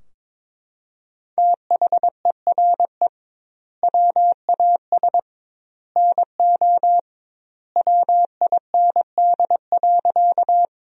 Morse code, heard as a keyed tone, about 22 wpm, 700 Hz. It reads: THERE WAS NO WIND.
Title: 2x Word-Spacing - Sentences from Top 400 Words - 22wpm